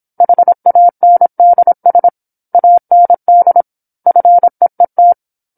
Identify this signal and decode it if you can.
Morse code, heard as a keyed tone, about 27 wpm, 700 Hz. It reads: HANDS AND FEET
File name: 08-Chapter-08 26 WPM